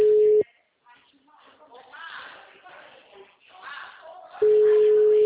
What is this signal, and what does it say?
Ringback tone (Europe, 425 Hz), with speech.